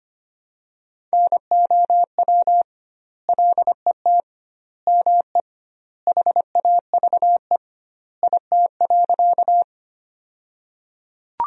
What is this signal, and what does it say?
Morse code, heard as a keyed tone, about 25 wpm, 700 Hz. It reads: NOW LET ME HAVE IT.
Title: 2x Word-Spacing - Sentences from Top 300 Words - 25wpm